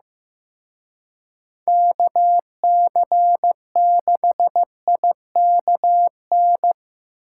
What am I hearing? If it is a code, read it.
Morse code, heard as a keyed tone, about 17 wpm, 700 Hz. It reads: KC6IKN
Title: Speed Racing - USA Call Signs 15wpm